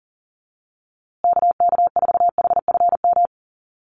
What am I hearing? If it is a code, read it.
Morse code, heard as a keyed tone, about 37 wpm, 700 Hz. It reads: KK4HFM
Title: USA Call Signs Repeated 3x 40wpm